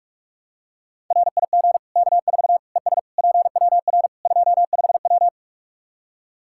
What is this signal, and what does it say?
Morse code, heard as a keyed tone, about 46 wpm, 700 Hz. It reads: RIG K4 ES PWR 25W
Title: Practice QSO - Exchange 2 - Info - 45wpm